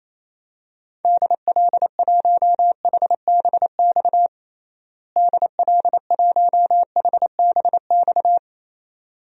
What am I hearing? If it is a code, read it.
Morse code, heard as a keyed tone, about 28 wpm, 700 Hz. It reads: DL1HBX DL1HBX
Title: DXCC-100 Call Signs - 6 Characters - Repeated 2x - 28wpm